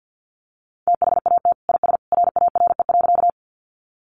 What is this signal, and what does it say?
Morse code, heard as a keyed tone, about 50 wpm, 700 Hz. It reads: THAT IS RARE.
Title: Sentences from Top 1100 Words 50wpm